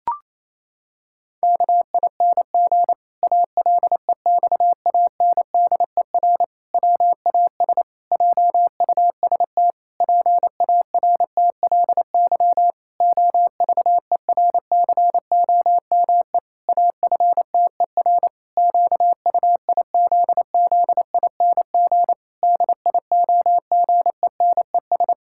Morse code, heard as a keyed tone, about 28 wpm, 700 Hz. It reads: KING ALEXANDER WAS JUST PARTLY OVERCOME AFTER QUIZZING DIOGENES
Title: Pangram Challenge v2 - 28wpm